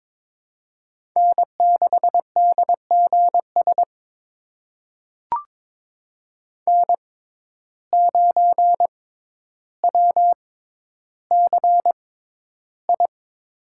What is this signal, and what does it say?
Morse code, heard as a keyed tone, about 22 wpm, spaced as for 11 wpm, 700 Hz. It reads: N6DGS N9WCI
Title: USA Call Signs - 5 Characters - 6x Character Spacing - 22wpm